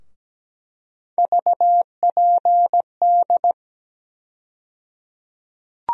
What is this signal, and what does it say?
Morse code, heard as a keyed tone, about 17 wpm, 700 Hz. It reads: VPD
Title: USA Call Sign Postfixes - 3 Letters 17wpm